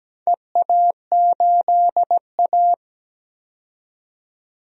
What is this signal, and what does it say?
Morse code, heard as a keyed tone, about 17 wpm, 700 Hz. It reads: EA8A
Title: DXCC-100 Call Signs - 4 Characters - 17wpm